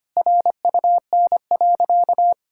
Morse code, heard as a keyed tone, about 25 wpm, 700 Hz. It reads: RUN.
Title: Sentences from Top 1000 Words - Review Part 6 - 25wpm